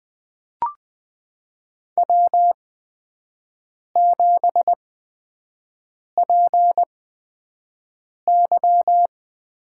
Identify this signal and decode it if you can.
Morse code, heard as a keyed tone, about 20 wpm, spaced as for 5 wpm, 700 Hz. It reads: W7PY
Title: USA Call Signs - 4 Characters - 8x Character Spacing - 20wpm